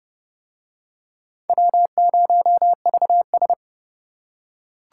Morse code, heard as a keyed tone, about 26 wpm, 700 Hz. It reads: W0VS